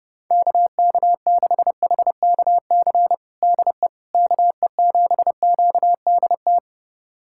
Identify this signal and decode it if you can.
Morse code, heard as a keyed tone, about 31 wpm, 700 Hz. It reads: KK6HKC DE KE7QDT
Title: USA Callsign DE Callsign - 6 Characters - 30wpm